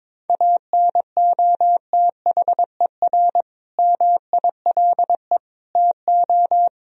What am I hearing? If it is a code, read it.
Morse code, heard as a keyed tone, about 22 wpm, 700 Hz. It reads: ANOTHER MILE TO
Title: Sentences from Top 1000 Words - Review Part 1 - 22wpm